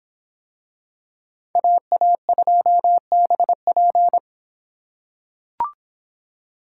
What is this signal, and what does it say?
Morse code, heard as a keyed tone, about 25 wpm, 700 Hz. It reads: AA2BP